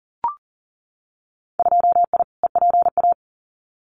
Morse code, heard as a keyed tone, about 40 wpm, 700 Hz. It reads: 2I EPA
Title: Winter Field Day Exchanges Encore 40wpm